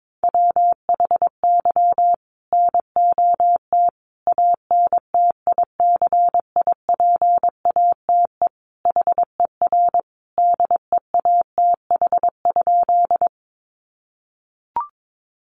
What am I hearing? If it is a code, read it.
Morse code, heard as a keyed tone, about 22 wpm, 700 Hz. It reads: WHY NOT ANTICIPATE HER DEATH?
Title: Sentences from Top 1500 Words 22wpm